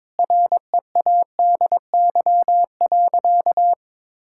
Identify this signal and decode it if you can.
Morse code, heard as a keyed tone, about 22 wpm, 700 Hz. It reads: READY.
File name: practice / 22wpm / Sentences with 3-Word Phrases - 22wpm